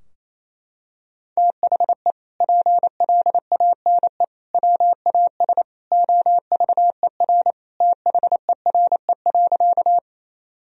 Morse code, heard as a keyed tone, about 27 wpm, 700 Hz. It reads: THE PLANE WAS OVER THERE.
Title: Sentences from Top 1000 Words - Review Part 3 - 28wpm